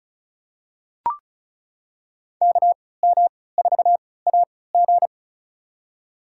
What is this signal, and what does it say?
Morse code, heard as a keyed tone, about 36 wpm, spaced as for 20 wpm, 700 Hz. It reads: KM4AG
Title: USA Call Signs - 5 Characters - 3x Character Spacing - 35wpm